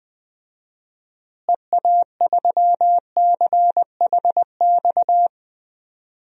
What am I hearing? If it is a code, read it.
Morse code, heard as a keyed tone, about 20 wpm, 700 Hz. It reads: EA3CHX
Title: DXCC-100 Call Signs - 6 Characters - Repeated 2x - 20wpm